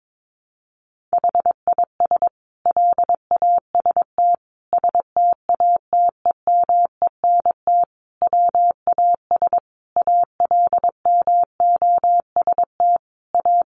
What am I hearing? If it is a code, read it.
Morse code, heard as a keyed tone, about 22 wpm, 700 Hz. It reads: HIS LAST STATEMENT WAS ALMOST A
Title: Sentences from Top 1200 Words 22wpm